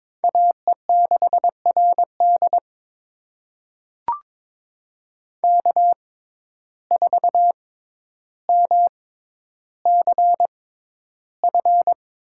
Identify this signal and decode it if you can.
Morse code, heard as a keyed tone, about 22 wpm, spaced as for 11 wpm, 700 Hz. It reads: AE6RD K4MCF
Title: USA Call Signs - 5 Characters - 6x Character Spacing - 22wpm